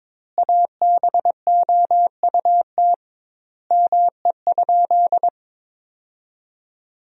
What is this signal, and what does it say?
Morse code, heard as a keyed tone, about 22 wpm, 700 Hz. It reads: ABOUT ME?
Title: 2x Word-Spacing - Sentences from Top 200 Words - 22wpm